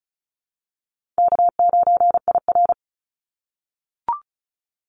Morse code, heard as a keyed tone, about 36 wpm, 700 Hz. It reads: K9IR
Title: NA SOTA Activator Callsigns 35wpm